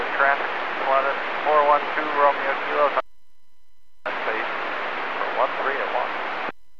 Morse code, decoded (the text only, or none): none